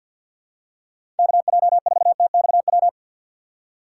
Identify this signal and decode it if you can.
Morse code, heard as a keyed tone, about 53 wpm, 700 Hz. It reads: KJ4TXW
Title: USA Call Signs - 6 Characters - 4x Character Spacing - 50wpm